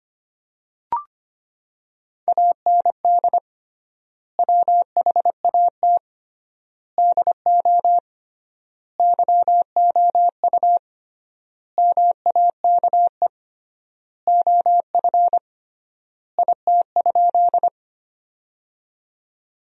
Morse code, heard as a keyed tone, about 25 wpm, 700 Hz. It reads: AND WHAT DO YOU MAKE OF IT?
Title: 3x Word-Spacing - Sentences from Top 100 Words - 25wpm